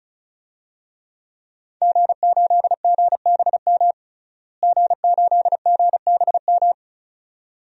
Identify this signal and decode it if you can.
Morse code, heard as a keyed tone, about 34 wpm, 700 Hz. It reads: G8GBM G8GBM
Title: DXCC-100 Call Signs - 5 Characters - Repeated 2x - 35wpm